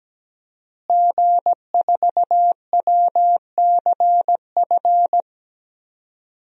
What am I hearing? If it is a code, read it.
Morse code, heard as a keyed tone, about 17 wpm, 700 Hz. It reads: G4WCF
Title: DXCC-100 Call Signs - 5 Characters - Repeated 3x - 17wpm